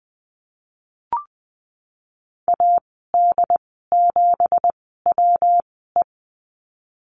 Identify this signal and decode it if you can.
Morse code, heard as a keyed tone, about 20 wpm, spaced as for 15 wpm, 700 Hz. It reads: AD7WE
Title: 167 Lesson 27g - Number 7 - Call Signs 20wpm